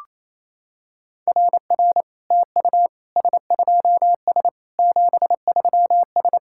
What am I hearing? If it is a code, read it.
Morse code, heard as a keyed tone, about 28 wpm, 700 Hz. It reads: RR TU S2S 73S
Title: SOTA QSO Practice 28wpm